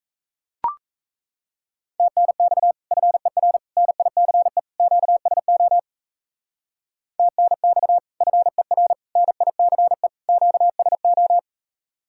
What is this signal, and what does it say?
Morse code, heard as a keyed tone, about 34 wpm, 700 Hz. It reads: TNX FER NICE QSO TNX FER NICE QSO
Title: Speed-Racing Common QSO Phrases Encore - 28wpm